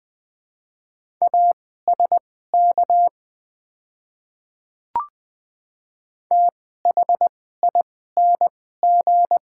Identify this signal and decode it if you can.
Morse code, heard as a keyed tone, about 20 wpm, spaced as for 14 wpm, 700 Hz. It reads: ASK THING
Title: 2x Character Spacing - Top 200 Words - 20wpm